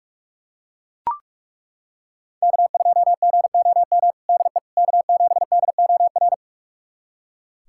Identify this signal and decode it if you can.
Morse code, heard as a keyed tone, about 42 wpm, 700 Hz. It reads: K2GOM DE K7DOR
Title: USA Callsign DE Callsign - 5 Characters - 45wpm